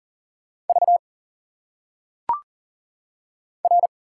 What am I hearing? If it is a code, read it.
Morse code, heard as a keyed tone, about 40 wpm, 700 Hz. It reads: V R
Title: Single Letters - Phonetic Pronunciation 40wpm